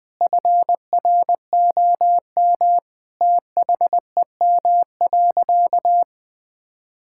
Morse code, heard as a keyed tone, about 20 wpm, 700 Hz. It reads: FROM THEM.
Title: Sentences from Top 1600 Words 20wpm